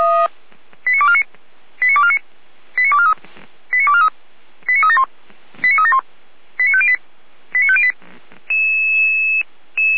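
Five-tone selective call (ZVEI), heard as quick runs of five tones, tones of about 70 ms. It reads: 89249 89249 89243 89243 89371 89371 89508 89508
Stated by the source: ZVEI five-tone selective call received by SDR.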